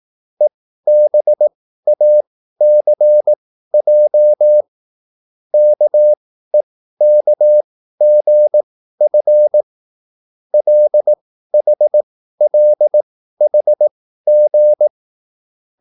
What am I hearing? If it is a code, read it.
Morse code, heard as a keyed tone, about 18 wpm, spaced as for 13 wpm, 600 Hz. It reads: EBACJ KEKGF LHLHG